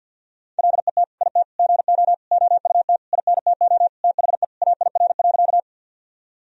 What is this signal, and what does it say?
Morse code, heard as a keyed tone, about 47 wpm, 700 Hz. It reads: LET IT GO OUT INTO THE AIR.
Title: Sentences from Top 1000 Words - Review Part 1 - 50wpm